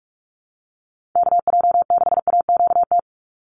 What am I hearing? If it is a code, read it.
Morse code, heard as a keyed tone, about 44 wpm, 700 Hz. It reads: KJ6AQT
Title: USA Call Signs 45wpm